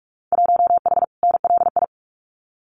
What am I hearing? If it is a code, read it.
Morse code, heard as a keyed tone, about 46 wpm, 700 Hz. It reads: 1H NLI
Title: Winter Field Day Exchanges 45wpm